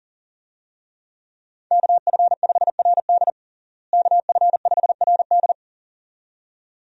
Morse code, heard as a keyed tone, about 40 wpm, 700 Hz. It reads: KF5RD KF5RD
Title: USA Call Signs - 5 Characters - Repeated 2x - 40wpm